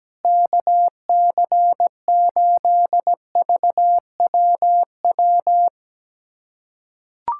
Morse code, heard as a keyed tone, about 17 wpm, 700 Hz. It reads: KC8VWW
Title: USA Callsign DE Callsign - 6 Characters - 17wpm